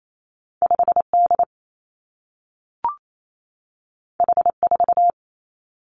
Morse code, heard as a keyed tone, about 28 wpm, 700 Hz. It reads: 5 D H 4
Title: Two Challenging Letter-Number Combinations 28wpm